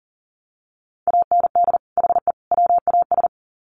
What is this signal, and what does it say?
Morse code, heard as a keyed tone, about 40 wpm, 700 Hz. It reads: AND HE WAS
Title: Sets of 3 Words 40wpm